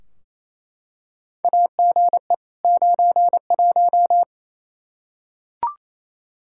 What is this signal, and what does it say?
Morse code, heard as a keyed tone, about 28 wpm, 700 Hz. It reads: AGE 91
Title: Practice QSO - Exchange 3 - More Info - 28wpm